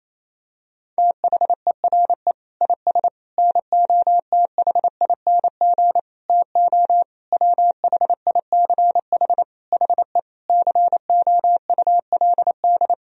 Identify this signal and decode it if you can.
Morse code, heard as a keyed tone, about 28 wpm, 700 Hz. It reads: THERE IS NOTHING TO WHICH HE COULD
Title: Sentences from Top 1000 Words - Review Part 3 - 28wpm